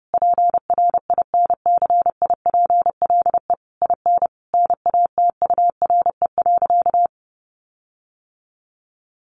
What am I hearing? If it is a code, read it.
Morse code, heard as a keyed tone, about 30 wpm, 700 Hz. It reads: PRINCIPLE IN NATURE.
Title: Sentences from Top 1400 Words 30wpm